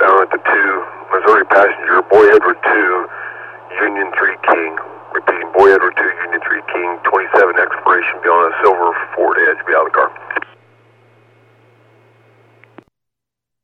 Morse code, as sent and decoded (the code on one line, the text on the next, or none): none